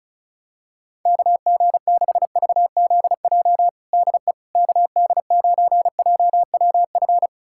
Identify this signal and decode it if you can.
Morse code, heard as a keyed tone, about 36 wpm, 700 Hz. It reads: KG6VZJ DE KD9JWF
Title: USA Callsign DE Callsign - 6 Characters - 35wpm